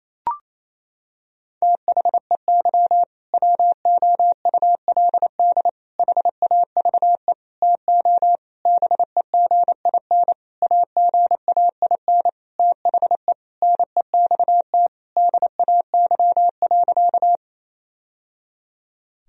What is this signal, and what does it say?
Morse code, heard as a keyed tone, about 28 wpm, 700 Hz. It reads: THEY WOULD HAVE TO BEGIN AGAIN THE NEXT DAY.
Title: Sentences from Top 1000 Words - Review Part 1 - 28wpm